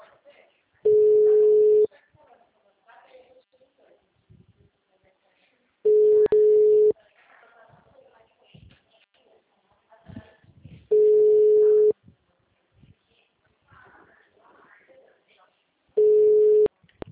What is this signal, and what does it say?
Ringback tone (Europe, 425 Hz), with speech.
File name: out- -2020- - - .45517